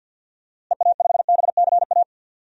Morse code, heard as a keyed tone, about 53 wpm, 700 Hz. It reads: EA5BCA
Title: DXCC-100 Call Signs - 6 Characters - Repeated 2x - 50wpm